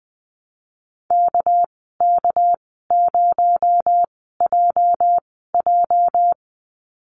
Morse code, heard as a keyed tone, about 20 wpm, spaced as for 15 wpm, 700 Hz. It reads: KK0JJ
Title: 219 Lesson 35g - Number 0 - Call Signs 20wpm